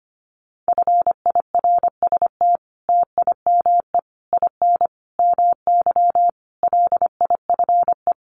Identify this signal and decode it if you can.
Morse code, heard as a keyed tone, about 25 wpm, 700 Hz. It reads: FIRST TIME IN MY LIFE
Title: Sets of 5 Words Encore2 25wpm